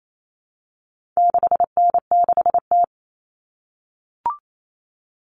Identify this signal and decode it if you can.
Morse code, heard as a keyed tone, about 28 wpm, 700 Hz. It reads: 6N6T
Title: All Cut Numbers - 4-Digit 28wpm